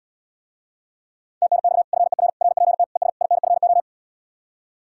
Morse code, heard as a keyed tone, about 80 wpm, 700 Hz. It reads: TNX FER RPT ES INFO